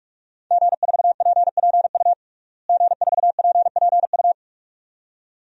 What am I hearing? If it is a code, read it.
Morse code, heard as a keyed tone, about 46 wpm, 700 Hz. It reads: G4PPU G4PPU
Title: DXCC-100 Call Signs - 5 Characters - Repeated 2x - 45wpm